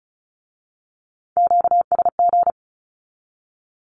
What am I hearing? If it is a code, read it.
Morse code, heard as a keyed tone, about 36 wpm, 700 Hz. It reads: QSG
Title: Q-Codes with Spoken Meaning - 35wpm